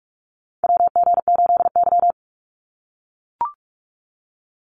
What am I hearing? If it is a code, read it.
Morse code, heard as a keyed tone, about 44 wpm, 700 Hz. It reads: WG8Y
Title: NA SOTA Chaser Callsigns 45wpm